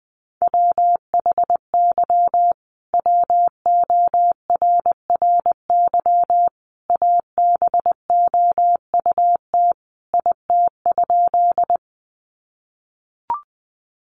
Morse code, heard as a keyed tone, about 20 wpm, 700 Hz. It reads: WHY WORRY ABOUT IT?
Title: Sentences from Top 1100 Words 20wpm